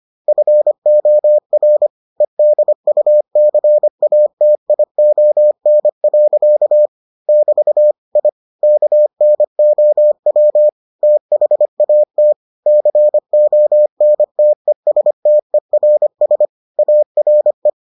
Morse code, heard as a keyed tone, about 25 wpm, 600 Hz. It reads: FOR EDUCATION. <BT> I KNOW THAT CONTESTERS ARE